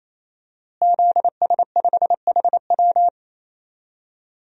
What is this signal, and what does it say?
Morse code, heard as a keyed tone, about 28 wpm, 700 Hz. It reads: ZS5HW
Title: DXCC-100 Call Signs - 5 Characters - Repeated 3x - 28wpm